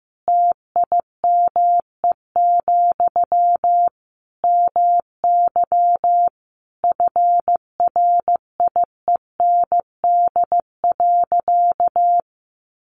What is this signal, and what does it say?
Morse code, heard as a keyed tone, about 15 wpm, 700 Hz. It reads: TIME, MY FRIEND.
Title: Sentences from Top 1100 Words 15wpm